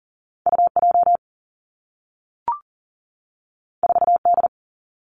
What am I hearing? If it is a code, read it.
Morse code, heard as a keyed tone, about 39 wpm, 700 Hz. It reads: UJ 4D
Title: Two Challenging Letter-Number Combinations 40wpm